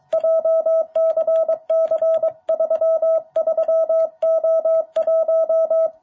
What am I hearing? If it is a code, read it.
Morse code, heard as a keyed tone, about 23 wpm, 660 Hz. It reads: JCC33O1
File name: jh8cbh / 2021cw